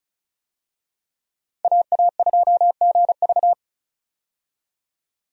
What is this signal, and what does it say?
Morse code, heard as a keyed tone, about 34 wpm, 700 Hz. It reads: AA2GV
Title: USA Call Signs - 5 Characters - 35wpm